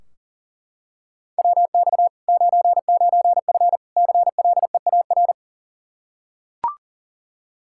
Morse code, heard as a keyed tone, about 40 wpm, 700 Hz. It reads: WX 99F CLEAR
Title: Practice QSO - Exchange 2 - Info - 40wpm